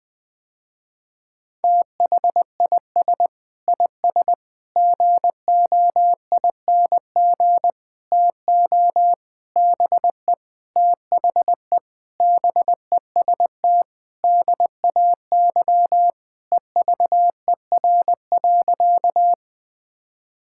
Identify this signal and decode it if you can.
Morse code, heard as a keyed tone, about 21 wpm, 700 Hz. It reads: THIS IS GOING TO BE THE BEST DAY EVER.
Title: Sentences with 5-Word Phrases - Spoken Beginning - 20wpm